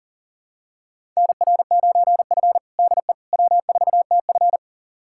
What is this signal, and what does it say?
Morse code, heard as a keyed tone, about 37 wpm, 700 Hz. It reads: NR9F DE W4TF
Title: USA Callsign DE Callsign - 4 Characters - 40wpm